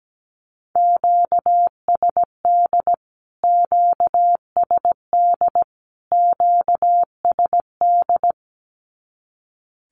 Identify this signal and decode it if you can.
Morse code, heard as a keyed tone, about 17 wpm, 700 Hz. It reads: QSD QSD QSD
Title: Q-Codes Repeated 3x 17wpm